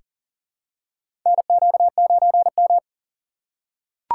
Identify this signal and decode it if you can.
Morse code, heard as a keyed tone, about 40 wpm, 700 Hz. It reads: NQ9M
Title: USA Call Signs - 2x Character Spacing - 40wpm